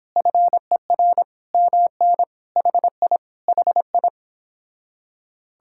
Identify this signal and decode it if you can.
Morse code, heard as a keyed tone, about 26 wpm, 700 Hz. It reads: FER MN HI HI